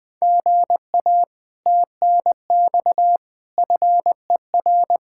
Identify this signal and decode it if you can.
Morse code, heard as a keyed tone, about 20 wpm, 700 Hz. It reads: GA TNX FER
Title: Speed-Racing Common QSO Phrases Encore - 20wpm